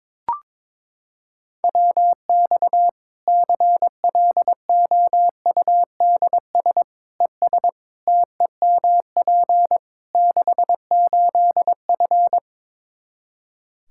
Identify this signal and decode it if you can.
Morse code, heard as a keyed tone, about 22 wpm, 700 Hz. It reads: WX CLOUDS ES TEMP 68F
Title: Practice QSO - Exchange 2 - Info - 22wpm